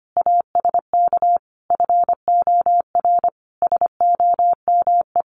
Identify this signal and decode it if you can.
Morse code, heard as a keyed tone, about 25 wpm, 700 Hz. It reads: ASK FOR SOME
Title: Sentences from Top 1500 Words 25wpm